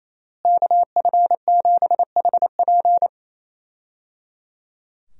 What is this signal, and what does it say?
Morse code, heard as a keyed tone, about 28 wpm, 700 Hz. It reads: KF7HP
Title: USA Call Signs - 5 Characters - Repeated 3x - 28wpm